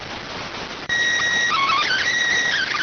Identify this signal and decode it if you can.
SSTV (slow-scan television), PD 290.